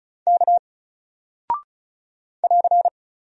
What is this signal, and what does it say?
Morse code, heard as a keyed tone, about 36 wpm, 700 Hz. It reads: K <AR>
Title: Single Character (Letter:Number:Punctuation:Prosign) Alt 35wpm